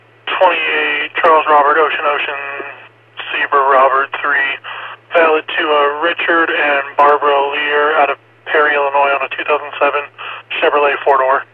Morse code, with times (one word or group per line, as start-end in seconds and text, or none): none